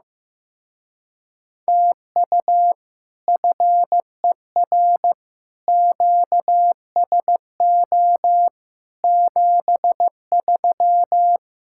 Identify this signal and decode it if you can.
Morse code, heard as a keyed tone, about 15 wpm, 700 Hz. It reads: TU FER QSO 73
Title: Speed-Racing Common QSO Phrases Encore - 15wpm